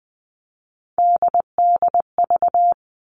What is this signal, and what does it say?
Morse code, heard as a keyed tone, about 20 wpm, 700 Hz. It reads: DDV
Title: All Cut Numbers - 3-Digit 20wpm